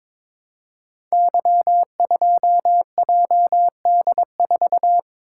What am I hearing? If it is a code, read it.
Morse code, heard as a keyed tone, about 22 wpm, 700 Hz. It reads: Y2JD4
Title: Code Groups - 5 Letter:Numbers - 22wpm